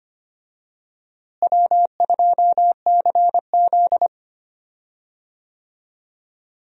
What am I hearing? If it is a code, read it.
Morse code, heard as a keyed tone, about 25 wpm, 700 Hz. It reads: W2CZ
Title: USA Call Signs - 4 Characters - Repeated 2x - 25wpm